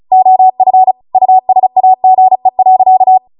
Morse code, heard as a keyed tone, about 36 wpm, 750 Hz. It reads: OF USAGE.